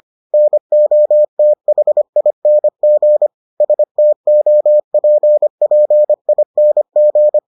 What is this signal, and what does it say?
Morse code, heard as a keyed tone, about 25 wpm, 600 Hz. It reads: NOTHING STOPPING